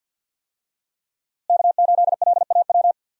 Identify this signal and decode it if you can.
Morse code, heard as a keyed tone, about 51 wpm, 700 Hz. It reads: K8LAW
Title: USA Call Signs - 5 Characters - Repeated 3x - 50wpm